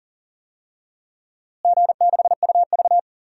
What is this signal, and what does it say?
Morse code, heard as a keyed tone, about 40 wpm, 700 Hz. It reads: G6UV
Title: DXCC-100 Call Signs - 4 Characters - Repeated 2x - 40wpm